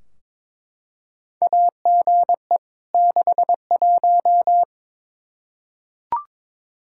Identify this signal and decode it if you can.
Morse code, heard as a keyed tone, about 22 wpm, 700 Hz. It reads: AGE 61
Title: Practice QSO - Exchange 3 - More Info - 22wpm